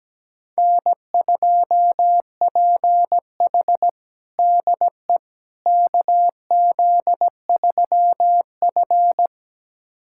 Morse code, heard as a keyed tone, about 17 wpm, 700 Hz. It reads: N2PH DE KZ3F
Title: USA Callsign DE Callsign - 4 Characters - 17wpm